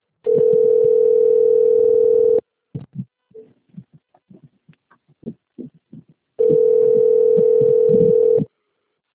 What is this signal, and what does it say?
Ringback tone (North America), 440 + 480 Hz.
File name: rg-1600-unknown- - - .66799